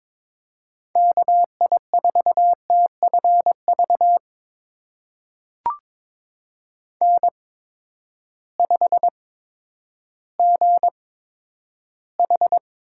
Morse code, heard as a keyed tone, about 22 wpm, spaced as for 11 wpm, 700 Hz. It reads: KI4TFV N5GH
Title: USA Call Signs - 8x Character Spacing - 22wpm